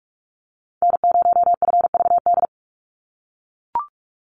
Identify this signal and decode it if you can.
Morse code, heard as a keyed tone, about 44 wpm, 700 Hz. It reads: N0FVD
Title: USA Call Signs 45wpm